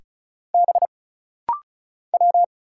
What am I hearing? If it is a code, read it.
Morse code, heard as a keyed tone, about 34 wpm, 700 Hz. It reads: B W
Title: Single Letters - Rapid-Fire 35wpm